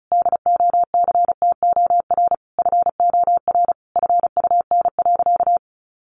Morse code, heard as a keyed tone, about 34 wpm, 700 Hz. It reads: DOCTOR FOR FUN.
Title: Additional Easy Phrases 35wpm